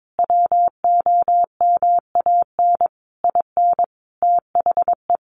Morse code, heard as a keyed tone, about 22 wpm, 700 Hz. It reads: WOMAN IN THE
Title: Sentences from Top 1100 Words 22wpm